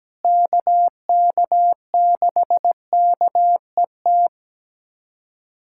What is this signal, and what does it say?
Morse code, heard as a keyed tone, about 17 wpm, 700 Hz. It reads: KK6KET
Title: USA Callsign DE Callsign - 6 Characters - 17wpm